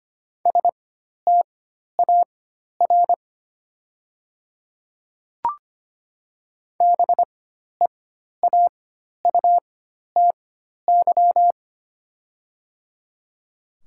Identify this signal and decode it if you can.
Morse code, heard as a keyed tone, about 25 wpm, spaced as for 12 wpm, 700 Hz. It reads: STAR BEAUTY
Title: 4x Character Spacing - Top 500 Words - 25wpm